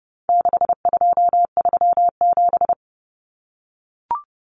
Morse code, heard as a keyed tone, about 30 wpm, 700 Hz. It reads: 6237
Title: Numbers - 4-Digit 30wpm